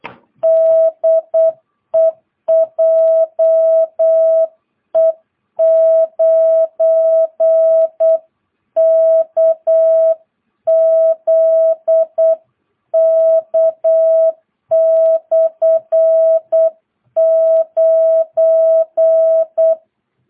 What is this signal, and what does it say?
Morse code, heard as a keyed tone, about 8 wpm, 650 Hz. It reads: DE JE9KZK/9
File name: jh8cbh / 2022cw